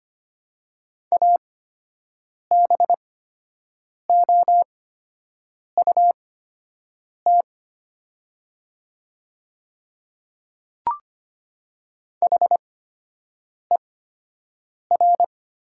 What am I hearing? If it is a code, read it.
Morse code, heard as a keyed tone, about 25 wpm, spaced as for 7 wpm, 700 Hz. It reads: ABOUT HER
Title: 8x Character Spacing - Top 100 Words - 25wpm